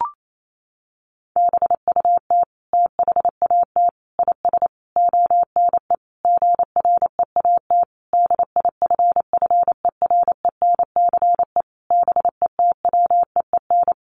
Morse code, heard as a keyed tone, about 28 wpm, 700 Hz. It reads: BUT THAT IS ONE GREAT DIFFERENCE BETWEEN
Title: Sentences from Top 800 Words 28wpm